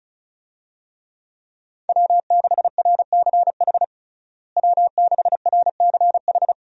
Morse code, heard as a keyed tone, about 36 wpm, 700 Hz. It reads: W6RCH W6RCH
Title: USA Call Signs - 5 Characters - Repeated 2x - 35wpm